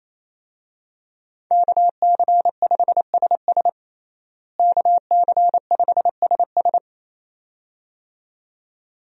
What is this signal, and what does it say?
Morse code, heard as a keyed tone, about 28 wpm, 700 Hz. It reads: KC5SS KC5SS
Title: USA Call Signs - 5 Characters - Repeated 2x - 28wpm